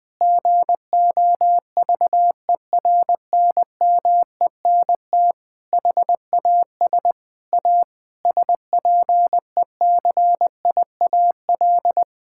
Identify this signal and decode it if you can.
Morse code, heard as a keyed tone, about 20 wpm, 700 Hz. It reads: GOVERNMENT HAS A SPECIAL
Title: Sentences from Top 1000 Words - Review Part 6 - 20wpm